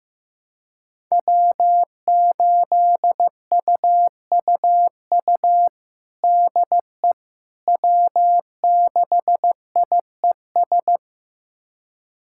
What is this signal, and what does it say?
Morse code, heard as a keyed tone, about 15 wpm, 700 Hz. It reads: W8UUU DE W6IES
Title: USA Callsign DE Callsign - 5 Characters - 15wpm